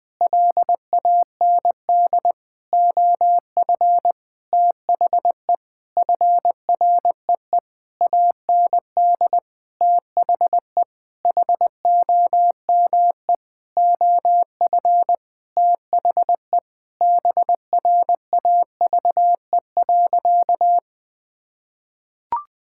Morse code, heard as a keyed tone, about 20 wpm, 700 Hz. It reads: LAND OF THE FREE AND THE HOME OF THE BRAVE.
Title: Sentences with 9-Word Phrases - Spoken Beginning - 20wpm